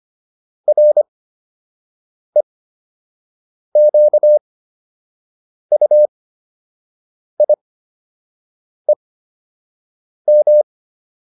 Morse code, heard as a keyed tone, about 25 wpm, spaced as for 6 wpm, 600 Hz. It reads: REQUIEM